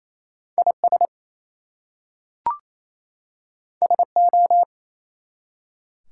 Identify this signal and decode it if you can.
Morse code, heard as a keyed tone, about 27 wpm, 700 Hz. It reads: IS SO
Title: Two Letter Words - 28wpm